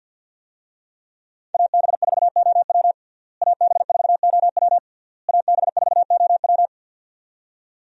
Morse code, heard as a keyed tone, about 50 wpm, 700 Hz. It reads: AB4OW AB4OW AB4OW
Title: USA Call Signs - 5 Characters - Repeated 3x - 50wpm